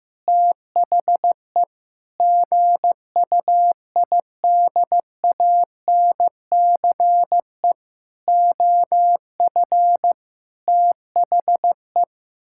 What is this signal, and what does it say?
Morse code, heard as a keyed tone, about 16 wpm, 700 Hz. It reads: THE GUIDANCE OF THE
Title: Sentences from Top 1600 Words 15wpm